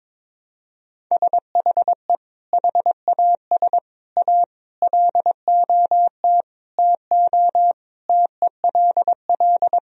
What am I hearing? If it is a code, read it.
Morse code, heard as a keyed tone, about 22 wpm, 700 Hz. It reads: SHE HAS A LOT TO TELL
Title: Sentences from Top 1000 Words - Review Part 3 - 22wpm